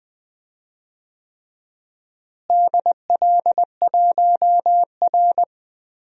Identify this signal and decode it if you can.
Morse code, heard as a keyed tone, about 20 wpm, 700 Hz. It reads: DL1R
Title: DXCC-100 Call Signs - 4 Characters - Repeated 3x - 20wpm